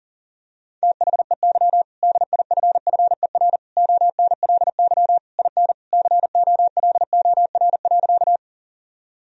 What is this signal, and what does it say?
Morse code, heard as a keyed tone, about 40 wpm, 700 Hz. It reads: THEY DIFFER ONLY IN COLOR.
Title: Sentences from Top 1000 Words - Review Part 2 - 40wpm